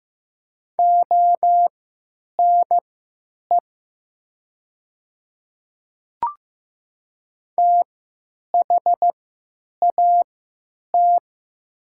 Morse code, heard as a keyed tone, about 15 wpm, spaced as for 9 wpm, 690 Hz. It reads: ONE THAT
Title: 3x Character Spacing - Top 100 Words - 15wpm